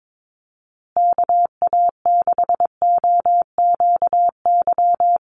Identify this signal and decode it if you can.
Morse code, heard as a keyed tone, about 22 wpm, 700 Hz. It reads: KA6OQY
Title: USA Call Signs Repeated 2x 22wpm